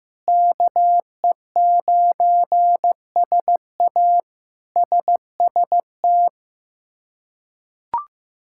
Morse code, heard as a keyed tone, about 16 wpm, 700 Hz. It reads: KE9SA SST
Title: K1USN SST QSO Practice - 15wpm